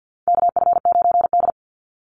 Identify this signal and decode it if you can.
Morse code, heard as a keyed tone, about 48 wpm, 700 Hz. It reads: KF9D
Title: Call Signs Repeated 2x - 50wpm